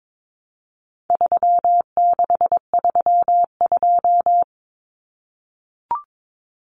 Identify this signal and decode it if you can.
Morse code, heard as a keyed tone, about 22 wpm, 700 Hz. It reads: 3632
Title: Common Cut Numbers - 4-Digit 22wpm